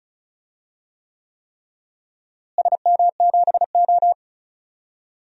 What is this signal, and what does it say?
Morse code, heard as a keyed tone, about 34 wpm, 700 Hz. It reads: SM7O
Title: DXCC-100 Call Signs - 4 Characters - Repeated 3x - 35wpm